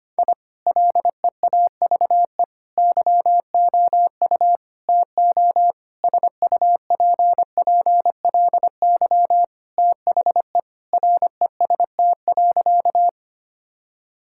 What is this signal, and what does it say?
Morse code, heard as a keyed tone, about 25 wpm, 700 Hz. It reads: I LEAVE YOU TO SUPPLY THE REST.
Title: Sentences from Top 1000 Words - Review Part 5 - 25wpm